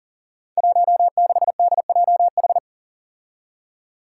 Morse code, heard as a keyed tone, about 39 wpm, 700 Hz. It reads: 16DJH
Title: Five Challenging Letter-Number Combinations 40wpm